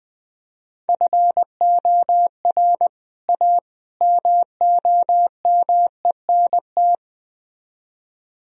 Morse code, heard as a keyed tone, about 20 wpm, 700 Hz. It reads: FOR A MOMENT
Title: Speed-Racing - Sets of 3 Words 17wpm